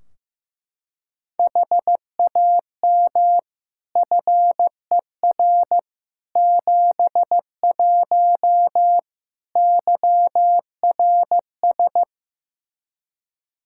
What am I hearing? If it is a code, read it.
Morse code, heard as a keyed tone, about 15 wpm, 700 Hz. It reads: HAM FER 71 YRS
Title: Practice QSO - Exchange 3 - More Info - 15wpm